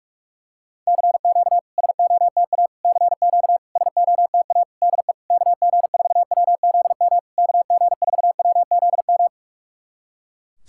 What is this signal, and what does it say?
Morse code, heard as a keyed tone, about 48 wpm, 700 Hz. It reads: CQ SOTA CQ SOTA DE KG4WZM KG4WZM
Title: SOTA QSO Practice 45wpm